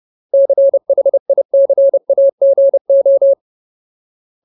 Morse code, heard as a keyed tone, about 30 wpm, 550 Hz. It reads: CHICAGO
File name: Ultimate Word List / 30WPM Ultimate